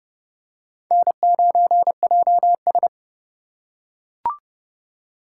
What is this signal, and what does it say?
Morse code, heard as a keyed tone, about 32 wpm, 700 Hz. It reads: N9JS
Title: USA Call Signs - 4 Characters - 2x Character Spacing - 30wpm